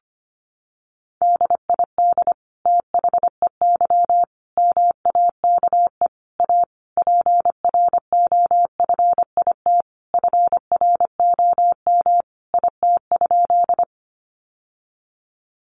Morse code, heard as a keyed tone, about 25 wpm, 700 Hz. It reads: DID THEY MAKE A PROFIT FROM IT?
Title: Sentences from Top 800 Words 25wpm